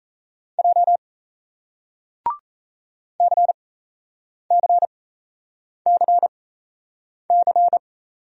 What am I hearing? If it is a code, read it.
Morse code, heard as a keyed tone, about 38 wpm, spaced as for 8 wpm, 700 Hz. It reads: J CCCC